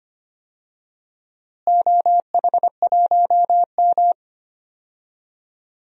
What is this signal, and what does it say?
Morse code, heard as a keyed tone, about 25 wpm, 700 Hz. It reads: OH1M
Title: DXCC-100 Call Signs - 4 Characters - Repeated 3x - 25wpm